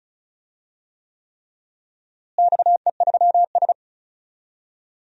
Morse code, heard as a keyed tone, about 34 wpm, 700 Hz. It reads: XE3S
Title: DXCC-100 Call Signs - 4 Characters - Repeated 3x - 35wpm